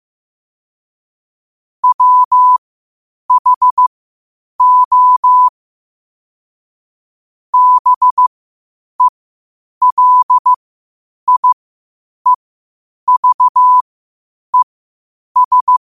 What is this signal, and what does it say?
Morse code, heard as a keyed tone, about 15 wpm, spaced as for 8 wpm, 1000 Hz. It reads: WHO BELIEVES